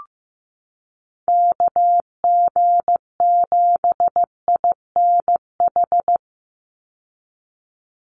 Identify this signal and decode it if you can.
Morse code, heard as a keyed tone, about 15 wpm, 700 Hz. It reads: KG7INH
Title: USA Call Signs 15wpm